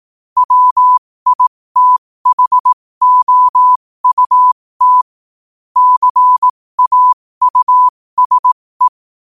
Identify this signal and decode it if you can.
Morse code, heard as a keyed tone, about 18 wpm, spaced as for 15 wpm, 1000 Hz. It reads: WITHOUT CAUSE
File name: qotdfiles / 15wpm